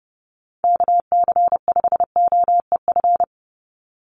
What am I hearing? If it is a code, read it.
Morse code, heard as a keyed tone, about 30 wpm, 700 Hz. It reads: KC5OEF
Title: USA Call Signs 30wpm